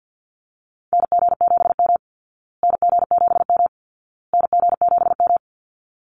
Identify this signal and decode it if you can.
Morse code, heard as a keyed tone, about 49 wpm, 700 Hz. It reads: NG7M NG7M NG7M
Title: Call Signs Repeated 2x - 50wpm